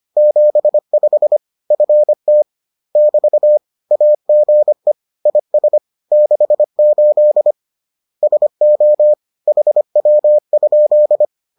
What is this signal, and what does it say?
Morse code, heard as a keyed tone, about 25 wpm, 600 Hz. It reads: 75 FT <BT> AGE IS 68 SO HW?